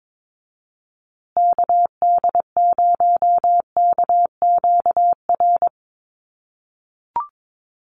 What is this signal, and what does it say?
Morse code, heard as a keyed tone, about 22 wpm, 700 Hz. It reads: KD0KQR
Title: USA Call Signs Repeated 2x 22wpm